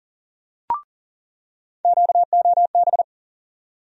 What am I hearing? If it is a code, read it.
Morse code, heard as a keyed tone, about 40 wpm, 700 Hz. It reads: QOB
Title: USA Call Sign Postfixes - 3 Letters 40wpm